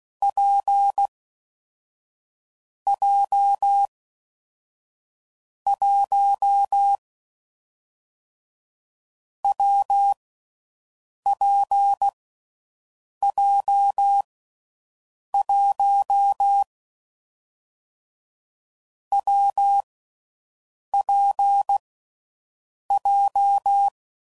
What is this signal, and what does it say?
Morse code, heard as a keyed tone, about 16 wpm, 800 Hz. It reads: P J 1 W P J 1 W P J
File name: Combinations / Difficult Combos / WPJ WPJ1